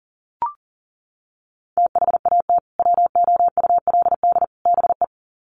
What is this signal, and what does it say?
Morse code, heard as a keyed tone, about 40 wpm, 700 Hz. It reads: THAT WOULD BE
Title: Sets of 3 Words 40wpm